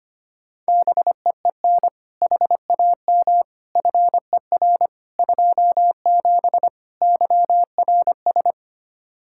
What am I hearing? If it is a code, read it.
Morse code, heard as a keyed tone, about 25 wpm, 700 Hz. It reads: BEEN HAM FER 27 YRS
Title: Practice QSO - Exchange 3 - More Info - 25wpm